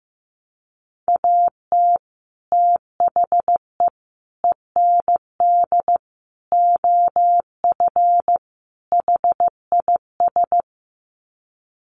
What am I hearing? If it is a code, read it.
Morse code, heard as a keyed tone, about 15 wpm, 700 Hz. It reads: AT THE END OF HIS
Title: Sets of 5 Words Encore 15wpm